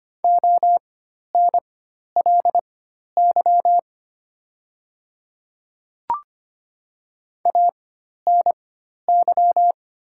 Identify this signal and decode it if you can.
Morse code, heard as a keyed tone, about 25 wpm, spaced as for 12 wpm, 690 Hz. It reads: ONLY ANY
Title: 4x Character Spacing - Top 200 Words - 25wpm